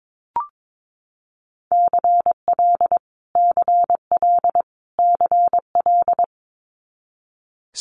Morse code, heard as a keyed tone, about 22 wpm, 700 Hz. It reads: CL CL CL
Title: Multi-letter Phonemes Repeated 3x 22wpm